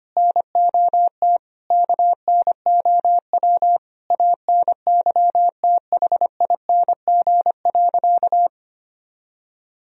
Morse code, heard as a keyed tone, about 25 wpm, 700 Hz. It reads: NOT KNOW ANYTHING.
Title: Sentences from Top 1000 Words - Review Part 2 - 25wpm